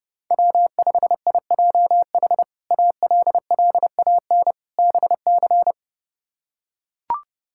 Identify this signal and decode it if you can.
Morse code, heard as a keyed tone, about 30 wpm, 700 Hz. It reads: W5IJH ALLAN BC
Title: K1USN SST QSO Practice - 30wpm